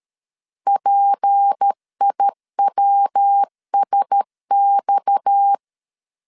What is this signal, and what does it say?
Morse code, heard as a keyed tone, about 13 wpm, 800 Hz. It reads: PIWSX
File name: LEZIONI / 6 PKX0 / 6 PKX 12wpm 2024